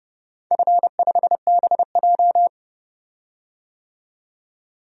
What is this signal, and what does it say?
Morse code, heard as a keyed tone, about 30 wpm, 700 Hz. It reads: F5BJ
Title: DXCC-100 Call Signs - 4 Characters - 30wpm